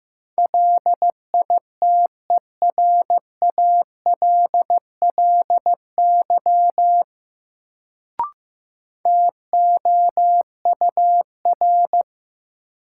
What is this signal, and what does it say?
Morse code, heard as a keyed tone, about 15 wpm, 700 Hz. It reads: LITERALLY TOUR
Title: Sentences from Top 1400 Words - New Word Drill - 15wpm